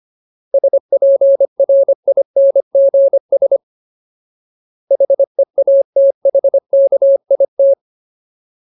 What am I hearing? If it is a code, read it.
Morse code, heard as a keyed tone, about 25 wpm, 550 Hz. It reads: SPRINGS HEATHKIT
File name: Ultimate Word List / 25WPM Ultimate